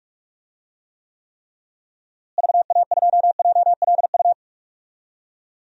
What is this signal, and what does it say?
Morse code, heard as a keyed tone, about 42 wpm, 700 Hz. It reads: VA2JLU
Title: DXCC-100 Call Signs - 6 Characters - Repeated 3x - 45wpm